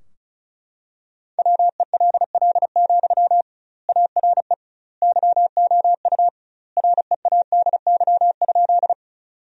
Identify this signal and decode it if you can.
Morse code, heard as a keyed tone, about 36 wpm, 700 Hz. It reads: WELL, ARE YOU READY?
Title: 2x Word-Spacing - Sentences from Top 400 Words - 35wpm